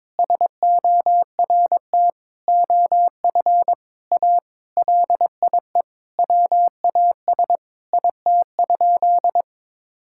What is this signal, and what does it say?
Morse code, heard as a keyed tone, about 22 wpm, 700 Hz. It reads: SORT OF A LIE WAS IT?
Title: Sentences from Top 1600 Words 22wpm